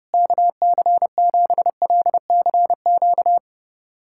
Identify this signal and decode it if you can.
Morse code, heard as a keyed tone, about 32 wpm, 700 Hz. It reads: KC7LCQ
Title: USA Callsign DE Callsign - 6 Characters - 30wpm